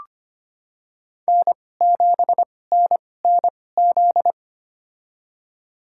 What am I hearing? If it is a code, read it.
Morse code, heard as a keyed tone, about 25 wpm, spaced as for 18 wpm, 700 Hz. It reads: N7NNZ
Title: USA Call Signs - 5 Characters - 2x Character Spacing - 25wpm